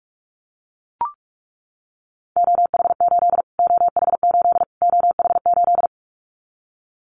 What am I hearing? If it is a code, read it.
Morse code, heard as a keyed tone, about 44 wpm, 700 Hz. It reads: OH8 OH8 OH8
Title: Callsign Prefixes for Top 25 DXCC - Repeated 3x - 45wpm